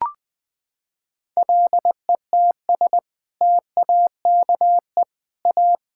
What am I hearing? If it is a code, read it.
Morse code, heard as a keyed tone, about 20 wpm, 700 Hz. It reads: LETS TAKE A
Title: Sets of 8 Words - 20wpm